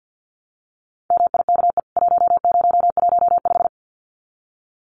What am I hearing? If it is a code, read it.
Morse code, heard as a keyed tone, about 50 wpm, 700 Hz. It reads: MIKE 1015
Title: CWT Exchange 50wpm